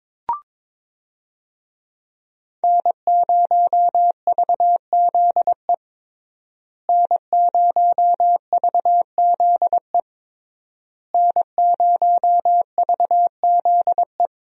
Morse code, heard as a keyed tone, about 22 wpm, 700 Hz. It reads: N0VZE N0VZE N0VZE
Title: USA Call Signs - 5 Characters - Repeated 3x - 22wpm